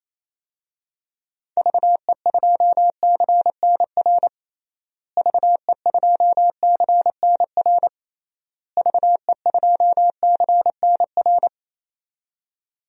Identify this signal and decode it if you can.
Morse code, heard as a keyed tone, about 28 wpm, 700 Hz. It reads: VE2CNR VE2CNR VE2CNR
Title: DXCC-100 Call Signs - 6 Characters - Repeated 3x - 28wpm